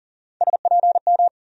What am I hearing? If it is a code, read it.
Morse code, heard as a keyed tone, about 40 wpm, 700 Hz. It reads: SPM
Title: USA Call Sign Postfixes - 3 Letters 40wpm